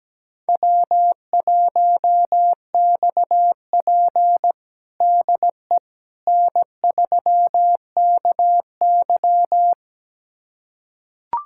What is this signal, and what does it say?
Morse code, heard as a keyed tone, about 17 wpm, 700 Hz. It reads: W1XP DE N3KY
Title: USA Callsign DE Callsign - 4 Characters - 17wpm